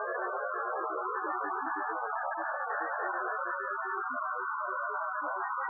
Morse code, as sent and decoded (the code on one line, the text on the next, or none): none